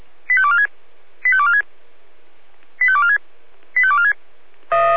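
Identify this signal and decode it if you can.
Five-tone selective call (ZVEI), heard as quick runs of five tones, tones of about 70 ms. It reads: 85246 85246 85246 85246
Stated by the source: ZVEI five-tone selective call received with an SDR.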